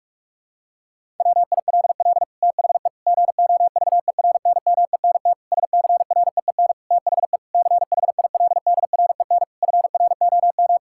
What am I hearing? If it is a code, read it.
Morse code, heard as a keyed tone, about 46 wpm, 700 Hz. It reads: WILL THE GOVERNMENT SCREEN THE CHILDREN FROM
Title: Sentences from Top 1000 Words - Review Part 6 - 45wpm